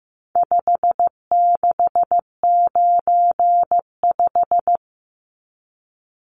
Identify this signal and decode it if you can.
Morse code, heard as a keyed tone, about 15 wpm, 700 Hz. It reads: 5695
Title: Numbers - 4-Digit 15wpm